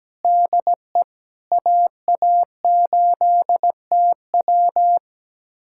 Morse code, heard as a keyed tone, about 17 wpm, 700 Hz. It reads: DE AA8TW
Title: USA Callsign DE Callsign 17wpm